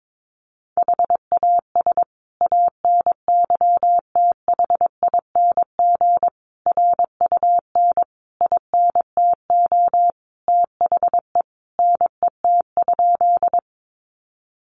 Morse code, heard as a keyed tone, about 22 wpm, 700 Hz. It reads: HAS ANYTHING RUN INTO THE NET?
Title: Sentences from Top 1400 Words 22wpm